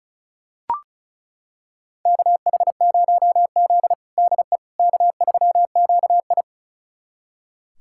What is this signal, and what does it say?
Morse code, heard as a keyed tone, about 34 wpm, 700 Hz. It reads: KH0Z DE K3QI
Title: USA Callsign DE Callsign - 4 Characters - 35wpm